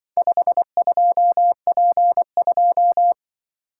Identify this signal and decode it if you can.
Morse code, heard as a keyed tone, about 24 wpm, 700 Hz. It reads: 52P2
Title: Speed-Racing - Four Letter-Number Combinations - 20wpm